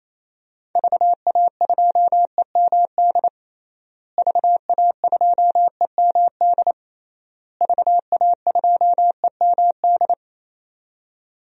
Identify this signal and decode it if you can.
Morse code, heard as a keyed tone, about 28 wpm, 700 Hz. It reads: VA2EMD VA2EMD VA2EMD
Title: DXCC-100 Call Signs - 6 Characters - Repeated 3x - 28wpm